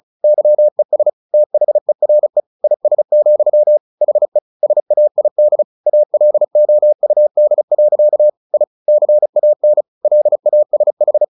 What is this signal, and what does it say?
Morse code, heard as a keyed tone, about 34 wpm, 600 Hz. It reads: YES THERE IS, HE SAID ALOUD. I CAN LASH
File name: the old man and the sea 2 35wpm